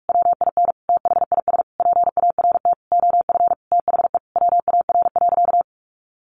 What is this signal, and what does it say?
Morse code, heard as a keyed tone, about 44 wpm, 700 Hz. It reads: WIN THIS PART OF THE WAR.
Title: Sentences from Top 1500 Words 45wpm